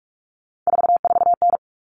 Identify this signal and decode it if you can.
Morse code, heard as a keyed tone, about 44 wpm, 700 Hz. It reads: 44N
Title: Common Cut Numbers - 3-Digit 45wpm